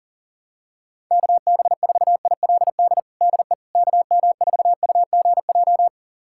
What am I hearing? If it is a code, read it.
Morse code, heard as a keyed tone, about 40 wpm, 700 Hz. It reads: KB4ILD DE KM4UGJ
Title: USA Callsign DE Callsign - 6 Characters - 40wpm